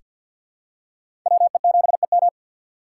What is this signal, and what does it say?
Morse code, heard as a keyed tone, about 48 wpm, 700 Hz. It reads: WE7EM
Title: USA Call Signs - 5 Characters - 4x Character Spacing - 50wpm